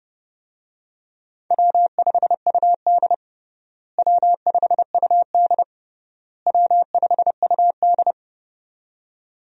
Morse code, heard as a keyed tone, about 30 wpm, 700 Hz. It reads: W5UD W5UD W5UD
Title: USA Call Signs - 4 Characters - Repeated 3x - 30wpm